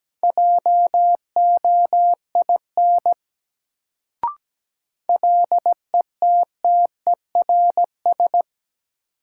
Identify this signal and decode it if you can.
Morse code, heard as a keyed tone, about 17 wpm, 700 Hz. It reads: JOIN LETTERS
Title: Sentences from Top 1000 Words - Review Part 6 - New Word Drill - 17wpm